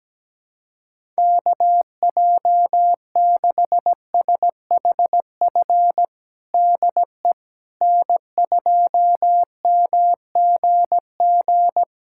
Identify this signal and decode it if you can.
Morse code, heard as a keyed tone, about 17 wpm, 700 Hz. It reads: KJ6SHF DE N2MGG
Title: USA Callsign DE Callsign 17wpm